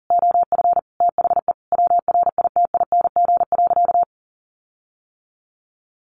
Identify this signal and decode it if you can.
Morse code, heard as a keyed tone, about 40 wpm, 700 Hz. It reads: OF THE WRITING.
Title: Sentences from Top 1400 Words 40wpm